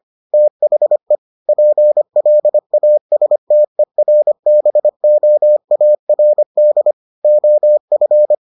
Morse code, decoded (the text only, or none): THE PLASTERBOARD OF